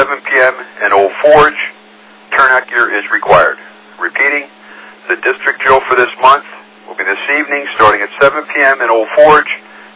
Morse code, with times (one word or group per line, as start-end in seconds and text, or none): none